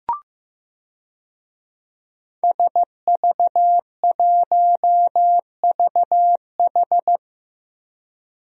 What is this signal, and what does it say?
Morse code, heard as a keyed tone, about 15 wpm, 700 Hz. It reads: SV1VH
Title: DXCC-100 Call Signs - 5 Characters - Repeated 2x - 15wpm